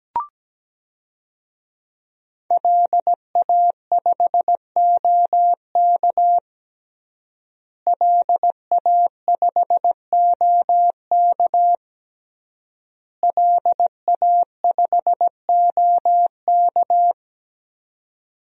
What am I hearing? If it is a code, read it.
Morse code, heard as a keyed tone, about 17 wpm, 700 Hz. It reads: LA5OK LA5OK LA5OK
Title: DXCC-100 Call Signs - 5 Characters - Repeated 3x - 17wpm